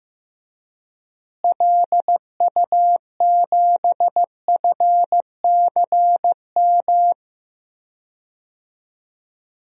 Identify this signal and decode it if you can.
Morse code, heard as a keyed tone, about 15 wpm, 700 Hz. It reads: LU7FCM
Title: DXCC-100 Call Signs - 6 Characters - Repeated 2x - 15wpm